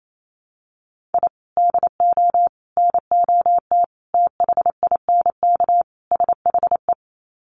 Morse code, heard as a keyed tone, about 28 wpm, 700 Hz. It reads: I DO NOT THINK SHE
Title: Sets of 5 Words 28wpm